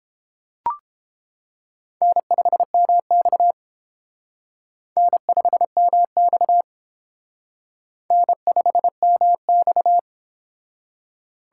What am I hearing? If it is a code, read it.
Morse code, heard as a keyed tone, about 28 wpm, 700 Hz. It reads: N5MX N5MX N5MX